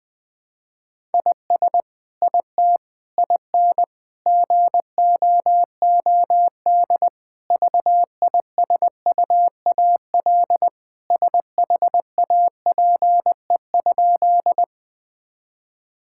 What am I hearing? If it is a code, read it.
Morse code, heard as a keyed tone, about 20 wpm, 700 Hz. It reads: IS IT IN GOOD VISUAL SHAPE?
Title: Sentences from Top 1000 Words - Review Part 6 - 20wpm